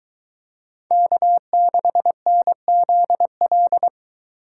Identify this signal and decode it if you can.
Morse code, heard as a keyed tone, about 23 wpm, 700 Hz. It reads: K6NZL